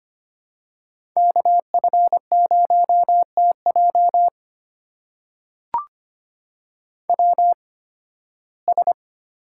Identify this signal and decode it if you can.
Morse code, heard as a keyed tone, about 25 wpm, spaced as for 16 wpm, 700 Hz. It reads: KF0TJ WS
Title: USA Call Signs - 8x Character Spacing - 25wpm